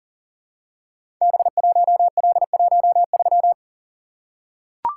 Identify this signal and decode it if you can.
Morse code, heard as a keyed tone, about 40 wpm, 700 Hz. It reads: B1L13
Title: Code Groups - 5 Letter:Numbers - 40wpm